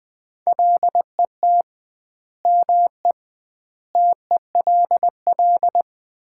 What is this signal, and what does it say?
Morse code, heard as a keyed tone, about 20 wpm, 700 Hz. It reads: LET ME TELL
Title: 2x Word-Spacing - Sets of 5 Words - 20wpm